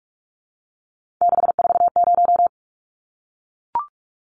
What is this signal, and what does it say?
Morse code, heard as a keyed tone, about 44 wpm, 700 Hz. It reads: 640
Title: Numbers - 3-Digit 45wpm